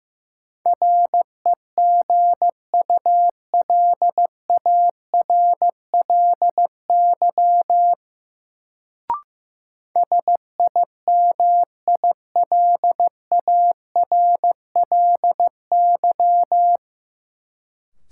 Morse code, heard as a keyed tone, about 15 wpm, 700 Hz. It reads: REGULARLY SIMILARLY
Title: Sentences from Top 1400 Words - New Word Drill - 15wpm